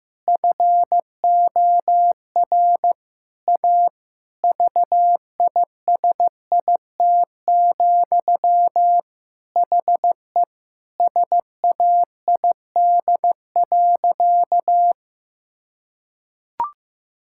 Morse code, heard as a keyed tone, about 15 wpm, 700 Hz. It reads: FOR A VISIT, HE SAID.
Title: Sentences from Top 1000 Words - Review Part 6 - 15wpm